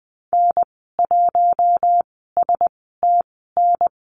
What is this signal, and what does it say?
Morse code, heard as a keyed tone, about 20 wpm, spaced as for 14 wpm, 700 Hz. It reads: N1STN
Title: 062 Lesson 11g - Letter h - Call Signs 20wpm